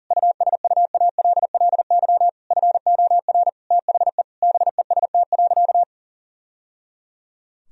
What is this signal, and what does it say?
Morse code, heard as a keyed tone, about 40 wpm, 700 Hz. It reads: USUALLY FOR THE BEST.
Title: Sentences from Top 1000 Words - Review Part 3 - 40wpm